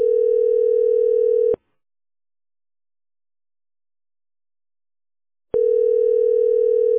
Ringback tone (North America), 440 + 480 Hz.